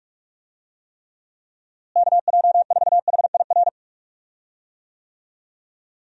Morse code, heard as a keyed tone, about 41 wpm, 700 Hz. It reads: KJ4HIR
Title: USA Call Signs - 6 Characters - Repeated 3x - 45wpm